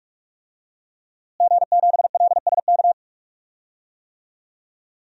Morse code, heard as a keyed tone, about 45 wpm, 700 Hz. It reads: G7LSK
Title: DXCC-100 Call Signs - 5 Characters - Repeated 2x - 45wpm